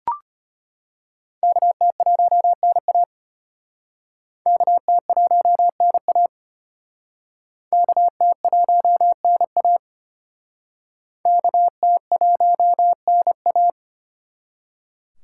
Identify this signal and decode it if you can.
Morse code, heard as a keyed tone, about 32 wpm, 700 Hz. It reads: KT1NA KT1NA KT1NA KT1NA